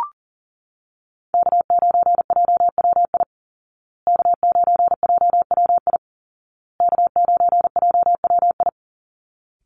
Morse code, heard as a keyed tone, about 40 wpm, 700 Hz. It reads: K9JWI K9JWI K9JWI
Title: Call Signs Repeated 2x - 40wpm